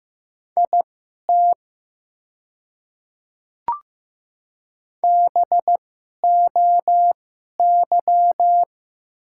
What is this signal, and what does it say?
Morse code, heard as a keyed tone, about 15 wpm, spaced as for 11 wpm, 690 Hz. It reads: IT BOY
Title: 2x Character Spacing - Top 300 Words - 15wpm